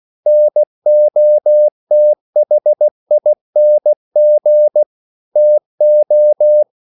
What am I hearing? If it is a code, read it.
Morse code, heard as a keyed tone, about 16 wpm, 600 Hz. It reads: NOTHING TO
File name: CW / Practice / EnglishQuotations0216